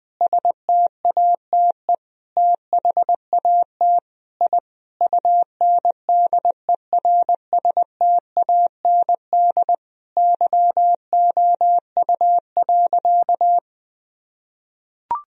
Morse code, heard as a keyed tone, about 20 wpm, 700 Hz. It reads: STATE THAT I UNDERSTAND YOU.
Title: Sentences from Top 1000 Words - Review Part 2 - 20wpm